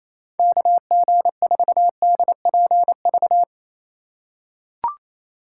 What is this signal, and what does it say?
Morse code, heard as a keyed tone, about 28 wpm, 700 Hz. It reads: KG4DPV
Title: USA Call Signs - 6x Character Spacing - 28wpm